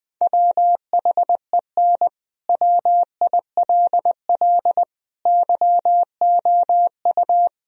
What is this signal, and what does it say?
Morse code, heard as a keyed tone, about 20 wpm, 700 Hz. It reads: WHEN WILL YOU
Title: Sentences from Top 1000 Words - Review Part 4 - 20wpm